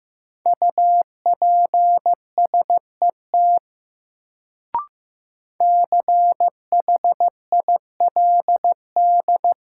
Morse code, heard as a keyed tone, about 15 wpm, 700 Hz. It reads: UPSET CHILD
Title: Sentences from Top 800 Words - New Word Drill - 15wpm